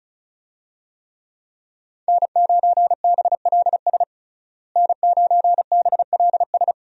Morse code, heard as a keyed tone, about 36 wpm, 700 Hz. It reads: N9BLS N9BLS
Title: USA Call Signs - 5 Characters - Repeated 2x - 35wpm